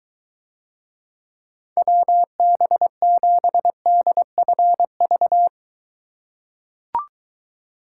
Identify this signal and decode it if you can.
Morse code, heard as a keyed tone, about 23 wpm, 700 Hz. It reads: WB7DFV